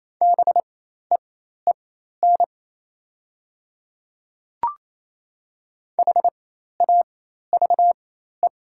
Morse code, heard as a keyed tone, about 28 wpm, spaced as for 13 wpm, 700 Hz. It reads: BEEN HAVE
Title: 4x Character Spacing - Top 100 Words - 28wpm